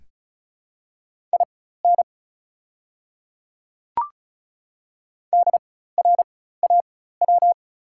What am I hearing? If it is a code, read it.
Morse code, heard as a keyed tone, about 36 wpm, spaced as for 16 wpm, 690 Hz. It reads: IN DRAW
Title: 4x Character Spacing - Top 300 Words - 35wpm